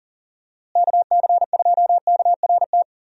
Morse code, heard as a keyed tone, about 40 wpm, 700 Hz. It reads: KC2KRT
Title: USA Call Signs - 6 Characters - Repeated 2x - 40wpm